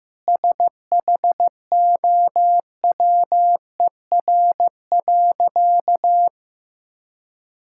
Morse code, heard as a keyed tone, about 15 wpm, 700 Hz. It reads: SHOWER.
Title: Sentences from Top 1600 Words 15wpm